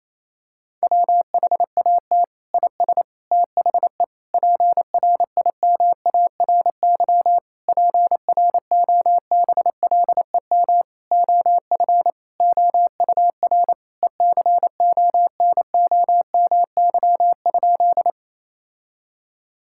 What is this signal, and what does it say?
Morse code, heard as a keyed tone, about 28 wpm, 700 Hz. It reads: WHAT IS THE PRIMARY PROBLEM OF OUR ECONOMY?
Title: Sentences from Top 1000 Words - Review Part 5 - 28wpm